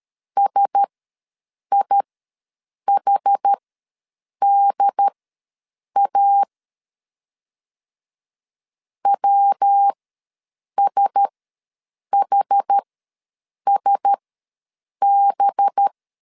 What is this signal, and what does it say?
Morse code, heard as a keyed tone, about 13 wpm, spaced as for 7 wpm, 800 Hz. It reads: SIHDA WSHSB
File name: 5 LB ws 12wpm